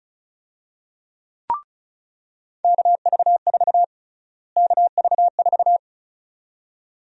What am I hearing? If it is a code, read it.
Morse code, heard as a keyed tone, about 36 wpm, 700 Hz. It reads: KV4 KV4
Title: USA Call Sign Prefixes - Repeated 2x - 35wpm